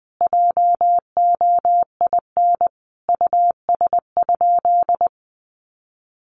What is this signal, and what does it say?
Morse code, heard as a keyed tone, about 20 wpm, 700 Hz. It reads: JOIN US?
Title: Sentences from Top 1400 Words 20wpm